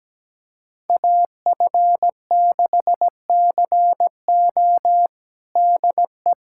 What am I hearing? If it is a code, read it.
Morse code, heard as a keyed tone, about 17 wpm, 700 Hz. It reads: AF6CO DE
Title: USA Callsign DE Callsign 17wpm